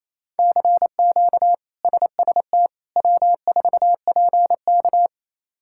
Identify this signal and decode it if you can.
Morse code, heard as a keyed tone, about 28 wpm, 700 Hz. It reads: CQ SST W4PK
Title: K1USN SST QSO Practice - 28wpm